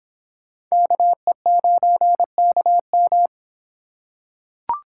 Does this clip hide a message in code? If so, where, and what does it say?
Yes: Morse code from 0.7 to 3.3 s, heard as a keyed tone, about 24 wpm, 700 Hz. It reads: KE9KM